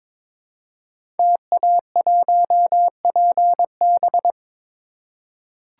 Morse code, heard as a keyed tone, about 21 wpm, 700 Hz. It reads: TA1PB
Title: Code Groups - 5 Letter:Numbers - 22wpm